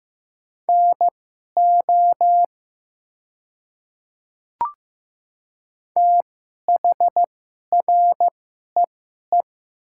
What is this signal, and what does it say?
Morse code, heard as a keyed tone, about 15 wpm, spaced as for 11 wpm, 700 Hz. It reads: NO THREE
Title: 2x Character Spacing - Top 200 Words - 15wpm